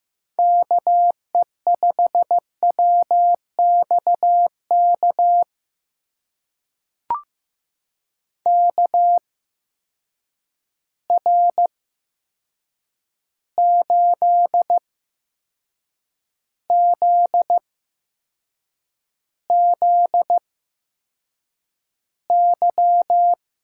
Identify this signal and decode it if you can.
Morse code, heard as a keyed tone, about 15 wpm, spaced as for 6 wpm, 700 Hz. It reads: KE5WXK KR8ZZY
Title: USA Call Signs - 6 Characters - 8x Character Spacing - 15wpm